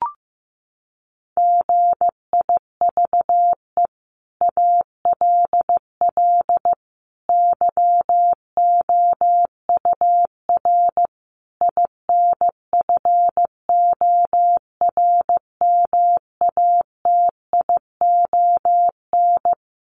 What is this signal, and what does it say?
Morse code, heard as a keyed tone, about 15 wpm, 700 Hz. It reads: GIVE ALL YOUR INFORMATION
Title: Sentences from Top 600 Words 15wpm